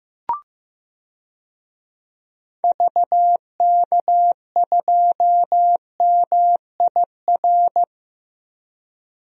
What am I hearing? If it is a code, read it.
Morse code, heard as a keyed tone, about 15 wpm, 700 Hz. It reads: VK2MIR
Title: DXCC-100 Call Signs - 6 Characters - Repeated 2x - 15wpm